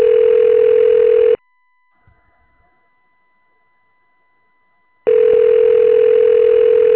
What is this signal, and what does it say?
Ringback tone (North America), 440 + 480 Hz.
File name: recordings / RE431d3524eb2cd585f2e4be807d3f9b41